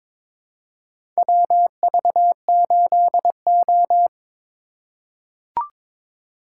Morse code, heard as a keyed tone, about 22 wpm, 700 Hz. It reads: WV8O
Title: USA Call Signs - 4 Characters - 3x Character Spacing - 22wpm